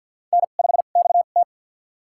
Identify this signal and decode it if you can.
Morse code, heard as a keyed tone, about 46 wpm, spaced as for 35 wpm, 700 Hz. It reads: N5XT
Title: USA Call Signs - 4 Characters - 2x Character Spacing - 50wpm